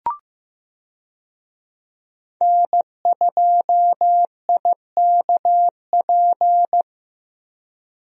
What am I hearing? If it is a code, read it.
Morse code, heard as a keyed tone, about 15 wpm, 700 Hz. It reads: N2IKP
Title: USA Call Signs - 5 Characters - Repeated 3x - 15wpm